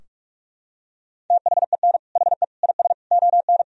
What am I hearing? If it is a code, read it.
Morse code, heard as a keyed tone, about 42 wpm, 700 Hz. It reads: THEN HE IS ON
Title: Sentences from Top 1000 Words - Review Part 4 - 45wpm